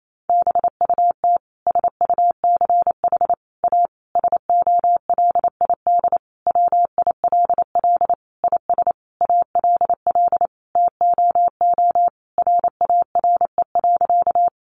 Morse code, heard as a keyed tone, about 28 wpm, 700 Hz. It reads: BUT SUCH A SOLID WILL IS ALL TOO RARE.
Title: Sentences from Top 1500 Words 28wpm